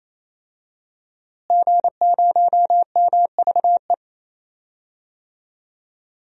Morse code, heard as a keyed tone, about 28 wpm, 700 Hz. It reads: G0MVE
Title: DXCC-100 Call Signs - 5 Characters - Repeated 3x - 28wpm